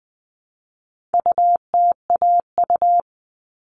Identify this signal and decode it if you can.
Morse code, heard as a keyed tone, about 20 wpm, 700 Hz. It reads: UTAU
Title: All Cut Numbers - 4-Digit 20wpm